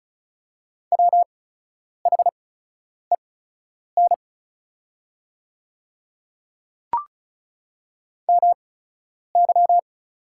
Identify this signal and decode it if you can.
Morse code, heard as a keyed tone, about 34 wpm, spaced as for 10 wpm, 690 Hz. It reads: WHEN MY
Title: 8x Character Spacing - Top 100 Words - 35wpm